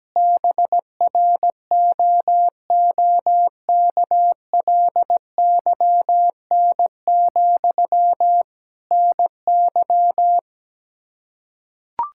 Morse code, heard as a keyed tone, about 17 wpm, 700 Hz. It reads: BROOKLYN, NY
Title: Practice QSO - Exchange 1 - Beginning - 17wpm